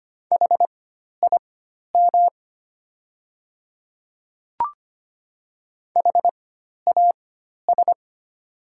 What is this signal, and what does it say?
Morse code, heard as a keyed tone, about 25 wpm, spaced as for 12 wpm, 700 Hz. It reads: HIM HAS
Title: 4x Character Spacing - Top 300 Words - 25wpm